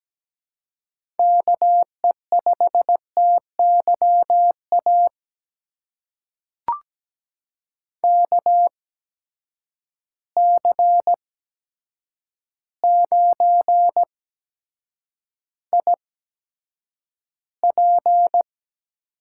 Morse code, heard as a keyed tone, about 17 wpm, spaced as for 8 wpm, 700 Hz. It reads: KE5TYA KC9IP
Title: USA Call Signs - 6 Characters - 8x Character Spacing - 17wpm